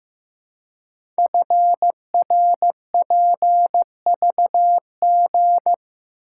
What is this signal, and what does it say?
Morse code, heard as a keyed tone, about 15 wpm, 700 Hz. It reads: FRPVG
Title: Code Groups - 5 Letters - 15wpm